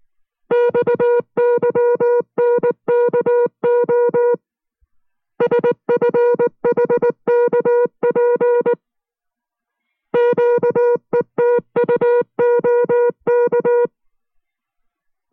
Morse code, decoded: XYNKO SFHKP QETUOK